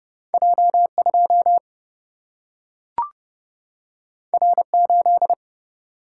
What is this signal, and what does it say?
Morse code, heard as a keyed tone, about 30 wpm, 700 Hz. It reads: J 2 R 8
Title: Single-Letter Single-Number 30wpm